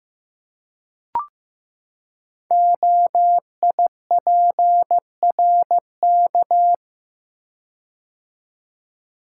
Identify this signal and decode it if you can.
Morse code, heard as a keyed tone, about 15 wpm, 700 Hz. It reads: OIPRK
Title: Code Groups - 5 Letters - 15wpm